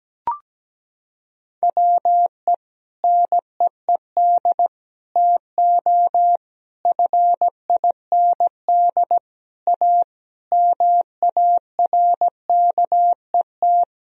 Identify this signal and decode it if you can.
Morse code, heard as a keyed tone, about 17 wpm, 700 Hz. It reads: WE NEED TO FIND A MARKET
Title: Sentences from Top 1000 Words - Review Part 3 - 17wpm